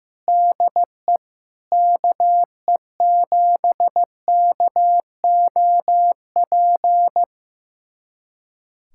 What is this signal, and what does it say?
Morse code, heard as a keyed tone, about 15 wpm, 700 Hz. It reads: DE KE7KOP
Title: USA Callsign DE Callsign - 6 Characters - 15wpm